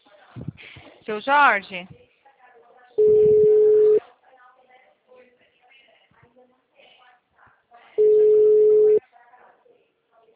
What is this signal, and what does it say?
Ringback tone (Europe, 425 Hz), with speech.
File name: out- -2030- - - .27136